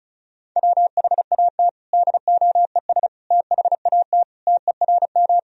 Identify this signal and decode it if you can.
Morse code, heard as a keyed tone, about 34 wpm, 700 Hz. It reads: WHAT DOES THAT TERM
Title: Sentences from Top 1000 Words - Review Part 4 - 35wpm